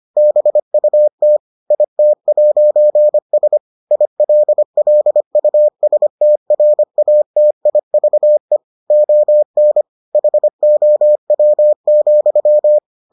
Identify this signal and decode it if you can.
Morse code, heard as a keyed tone, about 25 wpm, 600 Hz. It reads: BUT IT'S ILLUSTRATIVE ON HOW,